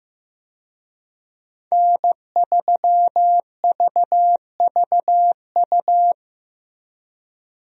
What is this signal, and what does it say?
Morse code, heard as a keyed tone, about 16 wpm, 700 Hz. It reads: N3VVU
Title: USA Call Signs - 5 Characters - Repeated 3x - 15wpm